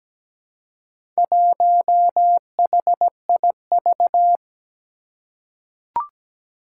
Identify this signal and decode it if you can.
Morse code, heard as a keyed tone, about 17 wpm, 700 Hz. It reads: 1HIV
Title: Four Challenging Letter-Number Combinations 17wpm